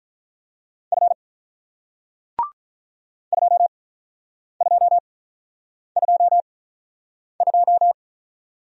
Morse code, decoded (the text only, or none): F 2222